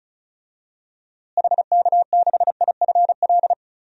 Morse code, heard as a keyed tone, about 36 wpm, 700 Hz. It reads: HK6IFL
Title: DXCC-100 Call Signs - 6 Characters - Repeated 2x - 35wpm